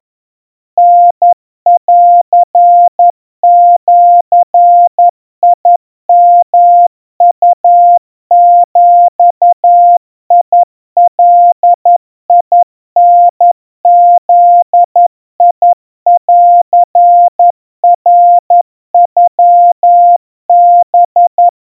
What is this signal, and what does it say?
Morse code (Japanese Wabun code), heard as a keyed tone, about 11 wpm, 700 Hz. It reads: タンジヨウビガタブンナノハ
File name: kawa / Morse sample / kamyu 40 cpm 700 Hz Crispy